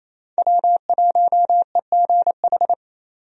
Morse code, heard as a keyed tone, about 27 wpm, 700 Hz. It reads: W1EGH
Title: USA Call Signs - 5 Characters - 28wpm